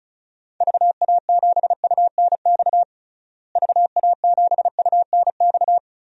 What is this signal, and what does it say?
Morse code, heard as a keyed tone, about 36 wpm, 700 Hz. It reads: VA7UNX VA7UNX
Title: DXCC-100 Call Signs - 6 Characters - Repeated 2x - 35wpm